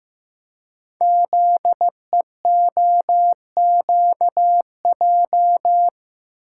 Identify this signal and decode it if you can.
Morse code, heard as a keyed tone, about 15 wpm, 700 Hz. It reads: ZEOQJ
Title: Code Groups - 5 Letters - 15wpm